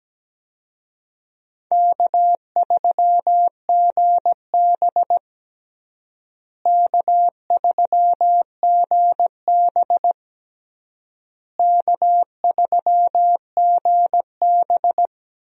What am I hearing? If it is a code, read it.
Morse code, heard as a keyed tone, about 17 wpm, 700 Hz. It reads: K3GB K3GB K3GB
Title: USA Call Signs - 4 Characters - Repeated 3x - 17wpm